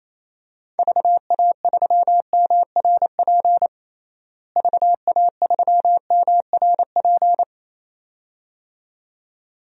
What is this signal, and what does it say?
Morse code, heard as a keyed tone, about 28 wpm, 700 Hz. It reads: VA3MRP VA3MRP
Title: DXCC-100 Call Signs - 6 Characters - Repeated 2x - 28wpm